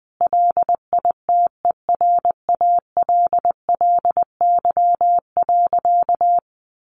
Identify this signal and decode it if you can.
Morse code, heard as a keyed tone, about 20 wpm, 700 Hz. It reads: LITERALLY.
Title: Sentences from Top 1400 Words 20wpm